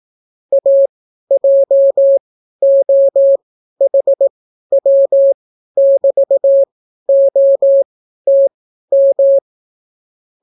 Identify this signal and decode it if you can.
Morse code, heard as a keyed tone, about 18 wpm, spaced as for 12 wpm, 550 Hz. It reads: AJOHW<BT>OTM